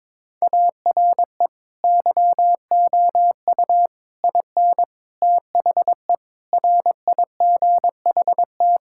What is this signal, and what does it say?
Morse code, heard as a keyed tone, about 22 wpm, 700 Hz. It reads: ARE YOU IN THE RIGHT
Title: Sentences from Top 1000 Words - Review Part 6 - 22wpm